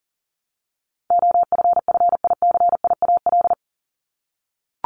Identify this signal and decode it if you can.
Morse code, heard as a keyed tone, about 40 wpm, 700 Hz. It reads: OFFICIAL
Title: Top 751-1000 Words 40wpm